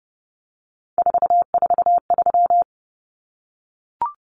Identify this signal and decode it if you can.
Morse code, heard as a keyed tone, about 30 wpm, 700 Hz. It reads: 443
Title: Common Cut Numbers - 3-Digit 30wpm